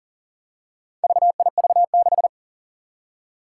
Morse code, heard as a keyed tone, about 41 wpm, 700 Hz. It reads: VIV6
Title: Four Challenging Letter-Number Combinations 40wpm